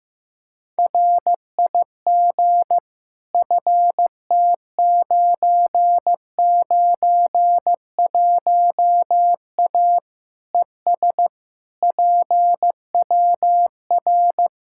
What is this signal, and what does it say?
Morse code, heard as a keyed tone, about 15 wpm, 700 Hz. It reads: RIG FT991A ES PWR
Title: K1USN SST QSO Practice 15wpm